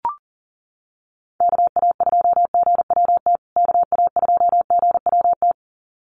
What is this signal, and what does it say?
Morse code, heard as a keyed tone, about 40 wpm, 700 Hz. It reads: KA2GWT KA2GWT
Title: USA Call Signs Repeated 2x 40wpm